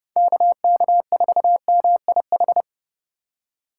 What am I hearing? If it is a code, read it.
Morse code, heard as a keyed tone, about 40 wpm, 700 Hz. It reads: KK4MIH